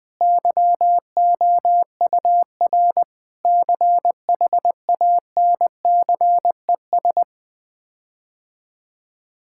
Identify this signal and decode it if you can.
Morse code, heard as a keyed tone, about 20 wpm, 700 Hz. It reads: YOUR CHANCES
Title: Sets of 8 Words - 20wpm